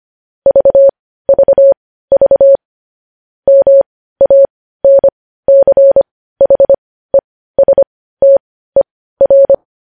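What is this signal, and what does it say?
Morse code, heard as a keyed tone, about 25 wpm, spaced as for 15 wpm, 550 Hz. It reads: VVV MANCHESTER
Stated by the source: News headlines sent in Morse code at 15 wpm.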